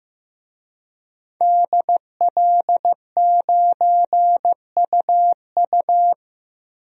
Morse code, heard as a keyed tone, about 15 wpm, 700 Hz. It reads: DL9UU
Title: DXCC-100 Call Signs - 5 Characters - 15wpm